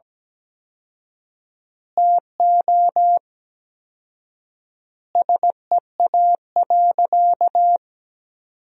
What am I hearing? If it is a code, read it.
Morse code, heard as a keyed tone, about 17 wpm, 700 Hz. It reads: TO SEA.
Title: 4x Word-Spacing - Sentences from Top 400 Words - 17wpm